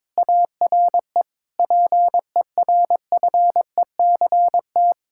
Morse code, heard as a keyed tone, about 22 wpm, 700 Hz. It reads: ARE PERFECT
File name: Difficult Sentences with Hints - 22wpm